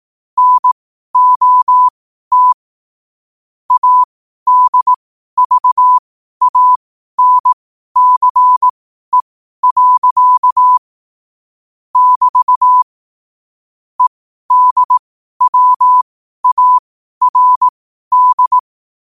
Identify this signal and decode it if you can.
Morse code, heard as a keyed tone, about 18 wpm, spaced as for 12 wpm, 1000 Hz. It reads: NOT ADVANCE. <BT> EDWARD
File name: qotdfiles / 12wpm